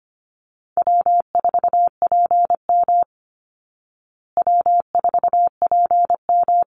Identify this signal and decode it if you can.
Morse code, heard as a keyed tone, about 25 wpm, 700 Hz. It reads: W4PM W4PM
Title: Easy Call Signs Repeated 2x - 25wpm